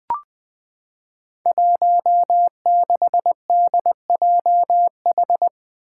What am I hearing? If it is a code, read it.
Morse code, heard as a keyed tone, about 20 wpm, 700 Hz. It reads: 16DJH
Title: Five Challenging Letter-Number Combinations 20wpm